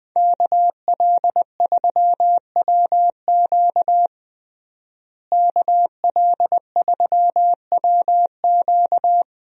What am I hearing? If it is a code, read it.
Morse code, heard as a keyed tone, about 20 wpm, 700 Hz. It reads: KL3WQ KL3WQ
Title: USA Call Signs - 5 Characters - Repeated 3x - 20wpm